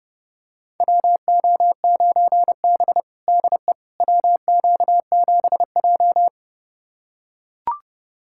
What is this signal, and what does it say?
Morse code, heard as a keyed tone, about 30 wpm, 700 Hz. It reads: WO9B DE WQ7J
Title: USA Callsign DE Callsign - 4 Characters - 30wpm